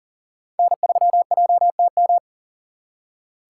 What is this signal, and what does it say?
Morse code, heard as a keyed tone, about 52 wpm, 700 Hz. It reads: N3JTM